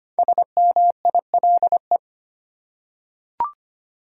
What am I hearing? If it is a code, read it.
Morse code, heard as a keyed tone, about 25 wpm, 700 Hz. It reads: SMILE
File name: 5 Letter Words - First 3 Letters Given - 25wpm